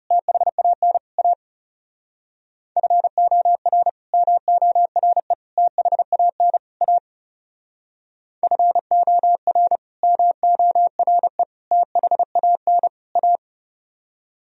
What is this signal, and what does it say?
Morse code, heard as a keyed tone, about 37 wpm, 700 Hz. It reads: THAN A FOR MORE THAN A FOR MORE THAN A
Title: Speed-Racing - Sets of 4 Words 30wpm